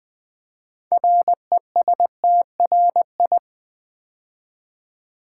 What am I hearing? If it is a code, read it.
Morse code, heard as a keyed tone, about 20 wpm, 700 Hz. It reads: RESTRI
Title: Build A Word - 2 Letters At A Time - 8-Letter Words - 20wpm